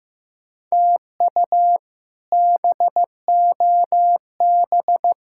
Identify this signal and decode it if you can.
Morse code, heard as a keyed tone, about 15 wpm, 700 Hz. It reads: TU BOB
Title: K1USN SST QSO Practice - 15wpm